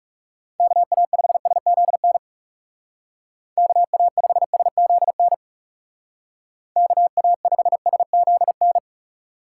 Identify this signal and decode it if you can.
Morse code, heard as a keyed tone, about 37 wpm, 700 Hz. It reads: KA5SZN KA5SZN KA5SZN
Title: Speed Racing - USA 2x3 Call Signs 30wpm